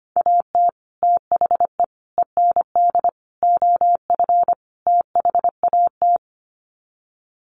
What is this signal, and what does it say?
Morse code, heard as a keyed tone, about 25 wpm, 700 Hz. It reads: AT THE END OF THAT
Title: Sets of 5 Words Encore2 25wpm